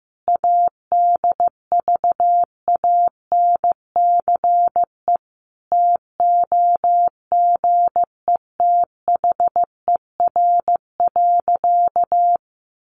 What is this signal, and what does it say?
Morse code, heard as a keyed tone, about 15 wpm, 700 Hz. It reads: ADVANCE TOGETHER.
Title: Sentences from Top 1400 Words 15wpm